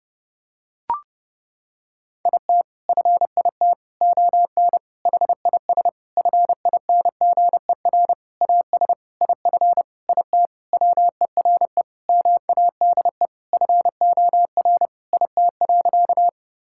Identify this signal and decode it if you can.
Morse code, heard as a keyed tone, about 30 wpm, 700 Hz. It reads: IT FIT ON HIS FINGER AS IF IT WERE MADE FOR IT.
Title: Sentences from Top 1600 Words 30wpm